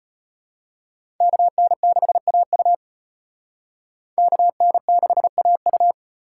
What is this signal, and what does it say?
Morse code, heard as a keyed tone, about 32 wpm, 700 Hz. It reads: KN6AU KN6AU